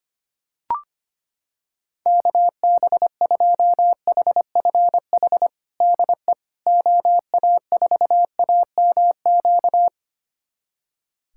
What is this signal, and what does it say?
Morse code, heard as a keyed tone, about 25 wpm, 700 Hz. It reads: KB2HFH DE OA4AMQ
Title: USA Callsign DE Callsign - 6 Characters - 25wpm